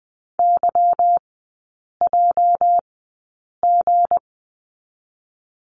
Morse code, heard as a keyed tone, about 20 wpm, spaced as for 8 wpm, 690 Hz. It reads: YJG
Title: 212 Lesson 34e - Number 8 - Three Characters 20wpm